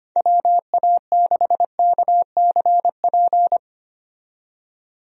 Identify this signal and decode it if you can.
Morse code, heard as a keyed tone, about 25 wpm, 700 Hz. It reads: WA6KCP
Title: USA Call Signs - 6 Characters - 25wpm